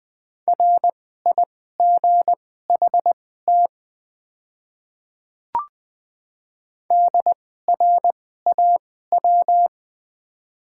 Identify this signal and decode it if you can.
Morse code, heard as a keyed tone, about 20 wpm, spaced as for 15 wpm, 700 Hz. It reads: RIGHT DRAW
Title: 2x Character Spacing - Top 500 Words - 20wpm